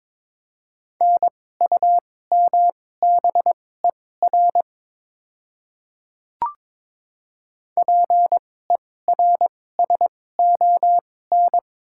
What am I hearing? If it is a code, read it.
Morse code, heard as a keyed tone, about 22 wpm, spaced as for 16 wpm, 700 Hz. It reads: NUMBER PERSON
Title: 2x Character Spacing - Top 500 Words - 22wpm